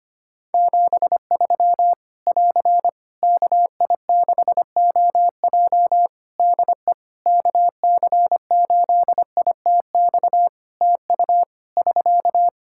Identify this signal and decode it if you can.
Morse code, heard as a keyed tone, about 25 wpm, 700 Hz. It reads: 73 <AR> KI6OJ DE KC8ITX TU <SK>
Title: K1USN SST QSO Practice 25wpm